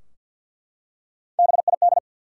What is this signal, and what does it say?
Morse code, heard as a keyed tone, about 47 wpm, 700 Hz. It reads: BID
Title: Three Letter Words - 50wpm